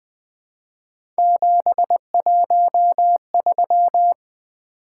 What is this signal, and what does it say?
Morse code, heard as a keyed tone, about 20 wpm, 700 Hz. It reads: 713
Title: Three Letter-Number Combinations - 20wpm